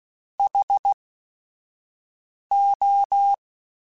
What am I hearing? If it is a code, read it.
Morse code, heard as a keyed tone, about 16 wpm, spaced as for 5 wpm, 800 Hz. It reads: HO
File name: T then H Runs / T then H Runs.mp3s / T then H Runs @5